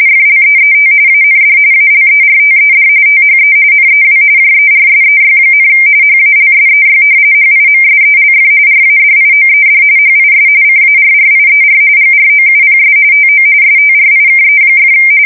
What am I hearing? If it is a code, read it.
RTTY, about 45.45 baud. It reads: FLEET AND ALERTED THE NEW JERSEY, HALSEY HIMSELF RADIOED HIS CARRIER GROUPS TO STRIKE!
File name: fsk170 / battleshipnj